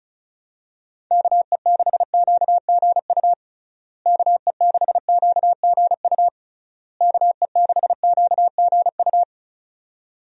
Morse code, heard as a keyed tone, about 34 wpm, 700 Hz. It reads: KE6QGU KE6QGU KE6QGU
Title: USA Call Signs - 6 Characters - Repeated 3x - 35wpm